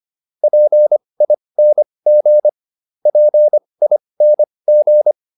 Morse code, heard as a keyed tone, about 25 wpm, spaced as for 20 wpm, 600 Hz. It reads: PING PING